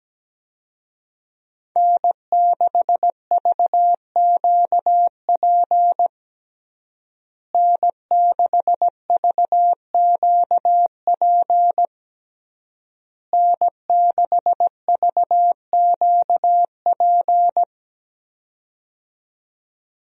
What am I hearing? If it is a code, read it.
Morse code, heard as a keyed tone, about 17 wpm, 700 Hz. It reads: N6VQP N6VQP N6VQP
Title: USA Call Signs - 5 Characters - Repeated 3x - 17wpm